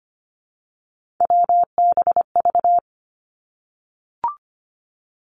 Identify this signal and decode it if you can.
Morse code, heard as a keyed tone, about 25 wpm, 700 Hz. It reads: WBV
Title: Three Challenging Letter-Number Combinations 25wpm